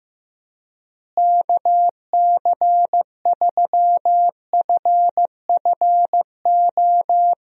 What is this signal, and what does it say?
Morse code, heard as a keyed tone, about 15 wpm, 700 Hz. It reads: KC3FFO
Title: USA Call Signs - 6 Characters - Repeated 3x - 15wpm